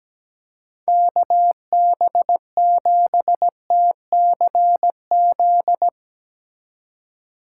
Morse code, heard as a keyed tone, about 20 wpm, 700 Hz. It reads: KB7TCZ
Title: Speed Racing - USA 2x3 Call Signs 17wpm